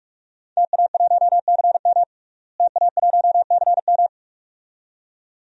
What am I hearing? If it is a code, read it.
Morse code, heard as a keyed tone, about 46 wpm, 700 Hz. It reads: TA1CM TA1CM
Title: DXCC-100 Call Signs - 5 Characters - Repeated 2x - 45wpm